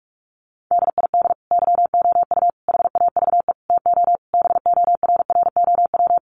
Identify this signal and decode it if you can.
Morse code, heard as a keyed tone, about 46 wpm, 700 Hz. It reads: DID YOU HAVE TO BORROW
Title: Sentences from Top 1500 Words 45wpm